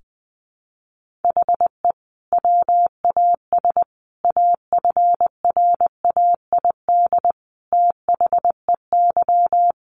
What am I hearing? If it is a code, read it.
Morse code, heard as a keyed tone, about 20 wpm, 700 Hz. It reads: HE WAS AFRAID THEY
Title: Sentences from Top 1500 Words 20wpm